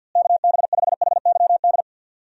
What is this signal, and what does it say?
Morse code, heard as a keyed tone, about 52 wpm, 700 Hz. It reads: KB5HYD
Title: USA Call Signs - 6 Characters - Repeated 2x - 50wpm